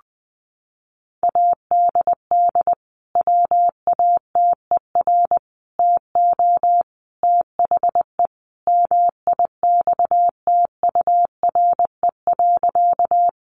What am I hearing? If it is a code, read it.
Morse code, heard as a keyed tone, about 20 wpm, 700 Hz. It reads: ADD WATER TO THE MIXTURE.
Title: Sentences from Top 1500 Words 20wpm